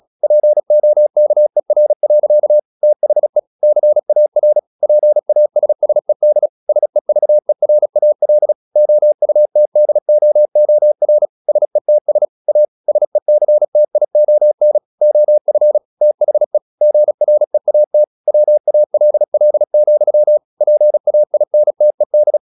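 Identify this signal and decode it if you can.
Morse code, heard as a keyed tone, about 36 wpm, 600 Hz. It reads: POKER. THE CAR PASSED SEVERAL OUTDOOR SETS A SECTION OF THE GREAT WALL, PAINTED
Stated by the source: Morse practice at 36 wpm, text of Ken Follett's novel Never, from CW med Gnister.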